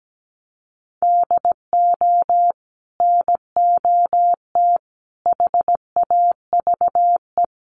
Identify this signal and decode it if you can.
Morse code, heard as a keyed tone, about 17 wpm, 700 Hz. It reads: DO NOT HAVE
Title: Sets of 4 Words 17wpm